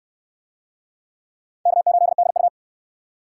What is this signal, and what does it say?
Morse code, heard as a keyed tone, about 68 wpm, 700 Hz. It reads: K8DU